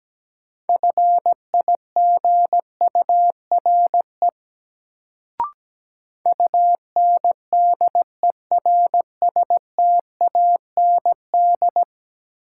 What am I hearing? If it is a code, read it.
Morse code, heard as a keyed tone, about 17 wpm, 700 Hz. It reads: FIGURE UNDERSTAND
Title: Sets of 4 Words - New Word Drill - 17wpm